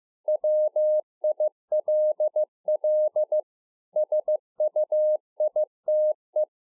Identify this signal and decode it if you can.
Morse code, heard as a keyed tone, about 15 wpm, 610 Hz. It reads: WILL SUITE